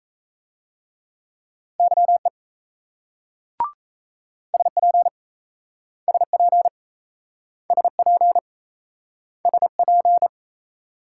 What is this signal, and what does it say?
Morse code, heard as a keyed tone, about 38 wpm, 700 Hz. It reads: YE SP SP SP SP